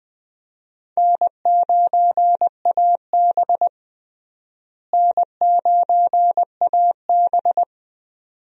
Morse code, heard as a keyed tone, about 21 wpm, 700 Hz. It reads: N9AB N9AB
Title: USA Call Signs - 4 Characters - Repeated 2x - 20wpm